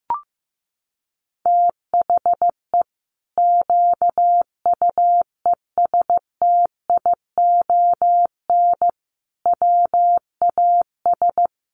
Morse code, heard as a keyed tone, about 15 wpm, 700 Hz. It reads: THE QUESTION WAS
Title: Sentences from Top 1200 Words 15wpm